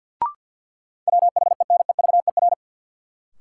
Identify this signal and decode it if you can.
Morse code, heard as a keyed tone, about 52 wpm, 700 Hz. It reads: WHENEVER
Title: Sentences from Top 1200 Words - New Word Drill - 50wpm